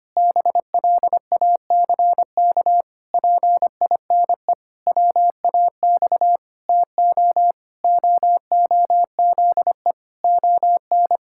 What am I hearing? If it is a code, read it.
Morse code, heard as a keyed tone, about 25 wpm, 700 Hz. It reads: BLACK PINE WAX TO OOZE ON
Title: Pangram Challenge v2 - 25wpm